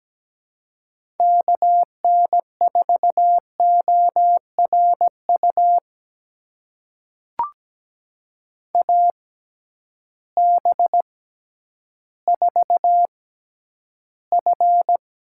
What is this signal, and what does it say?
Morse code, heard as a keyed tone, about 17 wpm, spaced as for 10 wpm, 700 Hz. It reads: KN4ORU AB4F
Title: USA Call Signs - 6x Character Spacing - 17wpm